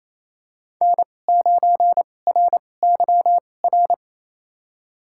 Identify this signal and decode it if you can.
Morse code, heard as a keyed tone, about 28 wpm, spaced as for 20 wpm, 700 Hz. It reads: N9RYR
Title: USA Call Signs - 5 Characters - 2x Character Spacing - 28wpm